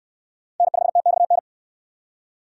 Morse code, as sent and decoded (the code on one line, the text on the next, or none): -. ..... - -..- -.
N5TXN